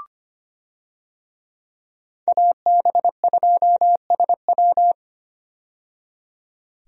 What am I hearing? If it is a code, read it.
Morse code, heard as a keyed tone, about 25 wpm, 700 Hz. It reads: AB2SW
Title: USA Call Signs - 5 Characters - 25wpm